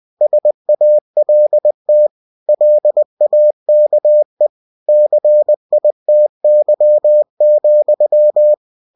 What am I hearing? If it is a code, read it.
Morse code, heard as a keyed tone, about 20 wpm, 600 Hz. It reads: SALT LAKE CITY,